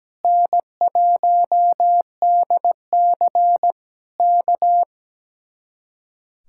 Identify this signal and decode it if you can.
Morse code, heard as a keyed tone, about 17 wpm, 700 Hz. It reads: N1DC K
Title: Speed-Racing Common QSO Phrases - 17wpm